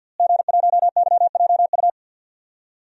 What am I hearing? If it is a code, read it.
Morse code, heard as a keyed tone, about 51 wpm, 700 Hz. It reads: G1YJU
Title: DXCC-100 Call Signs - 5 Characters - Repeated 3x - 50wpm